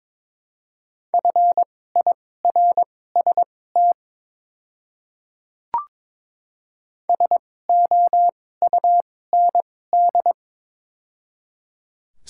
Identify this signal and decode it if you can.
Morse code, heard as a keyed tone, about 22 wpm, spaced as for 16 wpm, 700 Hz. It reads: FIRST SOUND
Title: 2x Character Spacing - Top 100 Words - 22wpm